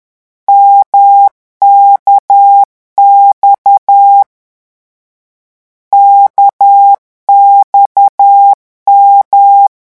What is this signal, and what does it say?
Morse code, heard as a keyed tone, about 10 wpm, 800 Hz. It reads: MKX KXM
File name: morse mp3 / lecon1-vitesse 10